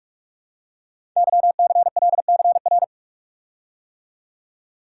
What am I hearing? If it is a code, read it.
Morse code, heard as a keyed tone, about 44 wpm, 690 Hz. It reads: YCLCR
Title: Code Groups - 5 Letters - 45wpm